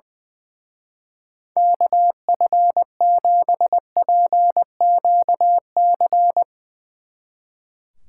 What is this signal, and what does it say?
Morse code, heard as a keyed tone, about 20 wpm, 700 Hz. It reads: KF7PQC
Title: Speed Racing - USA 2x3 Call Signs 20wpm